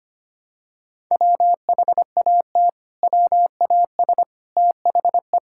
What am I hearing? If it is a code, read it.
Morse code, heard as a keyed tone, about 25 wpm, 700 Hz. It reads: WHAT WAS THE
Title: Sentences from Top 1600 Words 25wpm